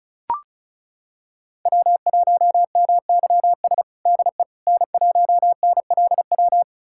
Morse code, heard as a keyed tone, about 34 wpm, 700 Hz. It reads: W1MYS DE N1NLW
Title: USA Callsign DE Callsign - 5 Characters - 35wpm